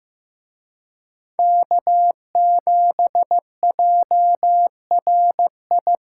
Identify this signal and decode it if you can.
Morse code, heard as a keyed tone, about 18 wpm, 700 Hz. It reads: K7JRI
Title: Speed Racing - USA 1x3 Call Signs 15wpm